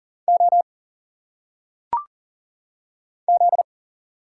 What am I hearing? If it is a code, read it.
Morse code, heard as a keyed tone, about 40 wpm, 700 Hz. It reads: O Z
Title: Single Letters - Phonetic Pronunciation 40wpm